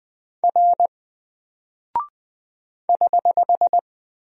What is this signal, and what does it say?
Morse code, heard as a keyed tone, about 20 wpm, 700 Hz. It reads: R <HH>
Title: Single Character (Letter:Number:Punctuation:Prosign) 20wpm